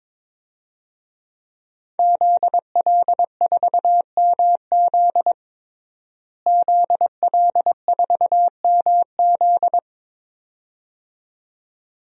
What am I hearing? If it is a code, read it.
Morse code, heard as a keyed tone, about 22 wpm, 700 Hz. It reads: ZL4MZ ZL4MZ
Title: DXCC-100 Call Signs - 5 Characters - Repeated 2x - 22wpm